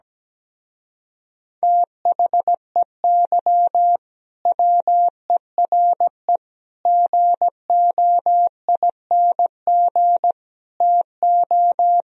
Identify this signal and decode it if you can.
Morse code, heard as a keyed tone, about 17 wpm, 700 Hz. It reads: THEY WERE GOING TO
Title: Speed-Racing - Sets of 4 Words 17wpm